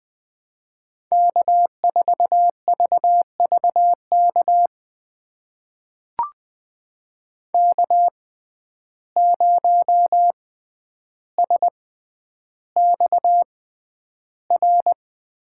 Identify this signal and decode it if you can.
Morse code, heard as a keyed tone, about 20 wpm, spaced as for 10 wpm, 700 Hz. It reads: K4VVK K0SXR
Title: USA Call Signs - 5 Characters - 6x Character Spacing - 20wpm